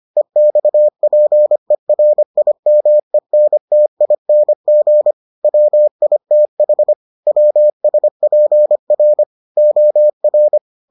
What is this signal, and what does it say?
Morse code, heard as a keyed tone, about 25 wpm, 600 Hz. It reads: EXPERIMENTING WITH WSPR OR